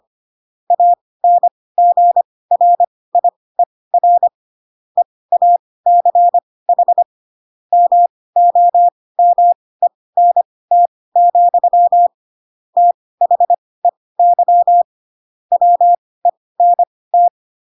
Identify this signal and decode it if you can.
Morse code, heard as a keyed tone, about 25 wpm, spaced as for 18 wpm, 700 Hz. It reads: ANGRIER EACH MOMENT, THEY WENT
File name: Pinocchio / Pinocchio ch2 12 25 ik1hge